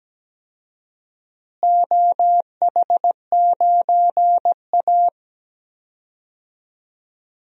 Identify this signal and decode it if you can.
Morse code, heard as a keyed tone, about 17 wpm, 700 Hz. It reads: OH9A
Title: DXCC-100 Call Signs - 4 Characters - Repeated 2x - 17wpm